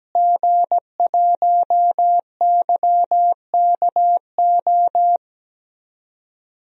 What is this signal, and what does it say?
Morse code, heard as a keyed tone, about 17 wpm, 700 Hz. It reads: G1YKO
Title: DXCC-100 Call Signs - 5 Characters - Repeated 3x - 17wpm